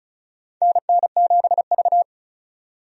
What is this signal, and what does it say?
Morse code, heard as a keyed tone, about 36 wpm, 700 Hz. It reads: NN7V
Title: USA Call Signs - 4 Characters - 35wpm